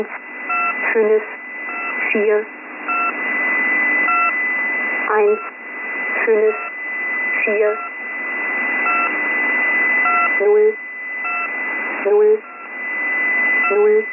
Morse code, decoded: none